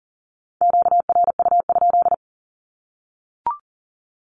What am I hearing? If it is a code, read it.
Morse code, heard as a keyed tone, about 40 wpm, 700 Hz. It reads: QRU?
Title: Q-Codes Repeated 3x 40wpm